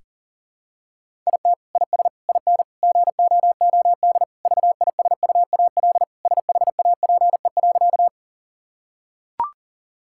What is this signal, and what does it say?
Morse code, heard as a keyed tone, about 40 wpm, 700 Hz. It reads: IT IS IN GOOD VISUAL SHAPE.
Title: Sentences from Top 1000 Words - Review Part 6 - 40wpm